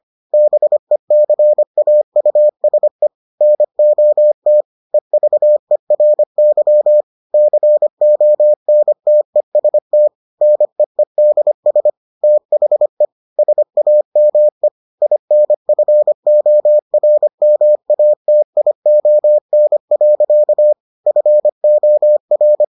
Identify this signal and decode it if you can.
Morse code, heard as a keyed tone, about 25 wpm, 600 Hz. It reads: BECAUSE NOT EVERY CONTEST NEEDS THE SAME INFORMATION. FOR